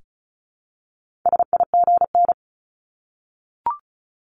Morse code, heard as a keyed tone, about 36 wpm, 700 Hz. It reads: SIGN
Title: Multi-letter Phonemes Repeated 3x 35wpm